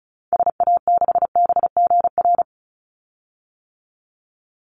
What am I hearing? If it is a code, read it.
Morse code, heard as a keyed tone, about 34 wpm, 700 Hz. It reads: SA6BGR
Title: Call Signs - 35wpm